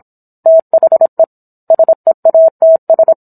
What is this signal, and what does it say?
Morse code, heard as a keyed tone, about 27 wpm, 650 Hz. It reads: THE SEATS